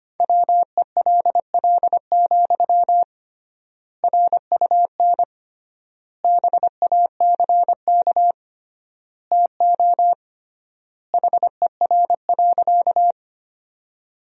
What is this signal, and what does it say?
Morse code, heard as a keyed tone, about 25 wpm, 700 Hz. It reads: WELL, RUN BACK TO HER.
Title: 3x Word-Spacing - Sentences from Top 300 Words - 25wpm